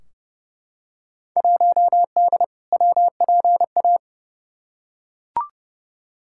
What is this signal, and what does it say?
Morse code, heard as a keyed tone, about 32 wpm, 700 Hz. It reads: 1D WPA
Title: Field Day Exchanges - 30wpm